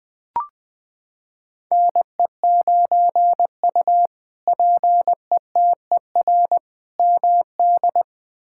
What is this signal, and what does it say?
Morse code, heard as a keyed tone, about 20 wpm, 700 Hz. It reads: NE9U PETER MD
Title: K1USN SST QSO Practice - 20wpm